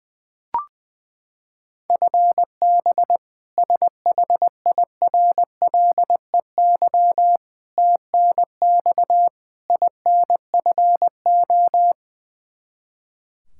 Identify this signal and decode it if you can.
Morse code, heard as a keyed tone, about 19 wpm, 700 Hz. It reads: FB SHIRLEY TNX INFO
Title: K1USN SST QSO Practice 20wpm